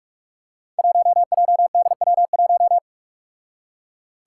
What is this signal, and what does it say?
Morse code, heard as a keyed tone, about 45 wpm, 700 Hz. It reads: 1JDW1
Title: Five Challenging Letter-Number Combinations 45wpm